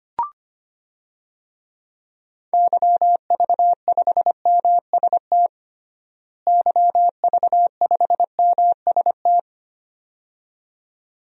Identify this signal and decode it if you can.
Morse code, heard as a keyed tone, about 25 wpm, 700 Hz. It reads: YV5MST YV5MST
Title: DXCC-100 Call Signs - 6 Characters - Repeated 2x - 25wpm